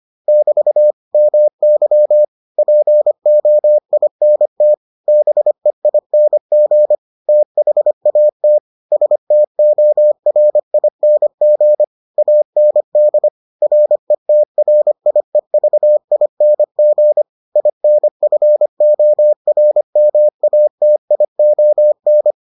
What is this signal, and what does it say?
Morse code, heard as a keyed tone, about 25 wpm, 600 Hz. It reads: <BT> MY POINT BEING THAT STORING AND RETRIEVING INFORMATION